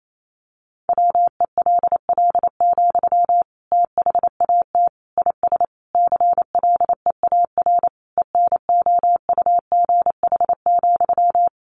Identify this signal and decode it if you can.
Morse code, heard as a keyed tone, about 28 wpm, 700 Hz. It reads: WELL, THAT IS CLEAR ENOUGH,
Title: Sentences from Top 900 Words 28wpm